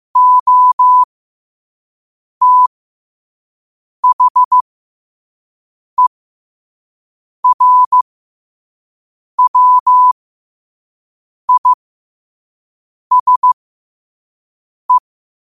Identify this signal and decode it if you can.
Morse code, heard as a keyed tone, about 15 wpm, spaced as for 5 wpm, 1000 Hz. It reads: OTHERWISE